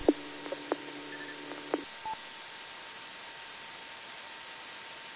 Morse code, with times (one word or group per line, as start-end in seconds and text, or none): none